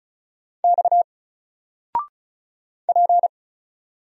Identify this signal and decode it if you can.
Morse code, heard as a keyed tone, about 36 wpm, 700 Hz. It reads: X P
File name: Single-Letter-Number-35wpm